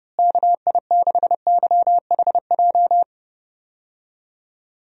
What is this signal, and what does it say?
Morse code, heard as a keyed tone, about 30 wpm, 700 Hz. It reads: KI6YHJ
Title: USA Call Signs - 6 Characters - Repeated 2x - 30wpm